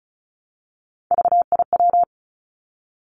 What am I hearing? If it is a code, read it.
Morse code, heard as a keyed tone, about 34 wpm, 700 Hz. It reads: VIW
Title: Three Challenging Letter-Number Combinations 35wpm